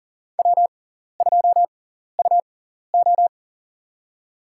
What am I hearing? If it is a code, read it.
Morse code, heard as a keyed tone, about 37 wpm, spaced as for 14 wpm, 700 Hz. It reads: W2UO
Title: USA Call Signs - 4 Characters - 6x Character Spacing - 40wpm